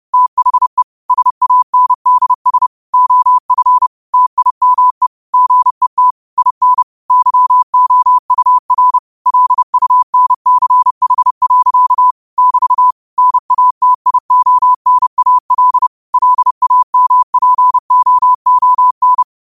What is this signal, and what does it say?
Morse code, heard as a keyed tone, about 30 wpm, 1000 Hz. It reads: THE SANDS OF TIME GET IN YOUR LUNCH. <BT> NATIONAL LAMPOON